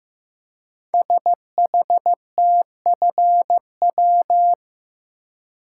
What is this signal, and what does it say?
Morse code, heard as a keyed tone, about 15 wpm, 700 Hz. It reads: SHTFW
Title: Code Groups - 5 Letters - 15wpm